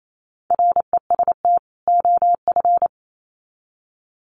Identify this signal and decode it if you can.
Morse code, heard as a keyed tone, about 28 wpm, 700 Hz. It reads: REST OF
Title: Sets of 2 Words 28wpm